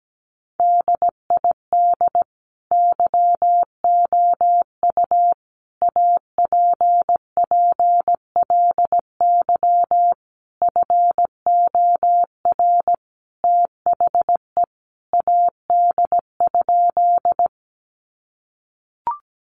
Morse code, heard as a keyed tone, about 17 wpm, 700 Hz. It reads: DID YOU APPLY FOR THE AD?
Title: Sentences from Top 1300 Words 17wpm